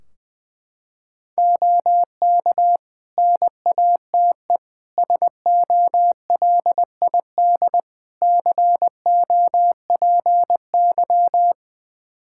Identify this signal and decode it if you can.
Morse code, heard as a keyed tone, about 20 wpm, 700 Hz. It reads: OK NATE SOLID COPY
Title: Practice QSO - Exchange 3 - More Info - 20wpm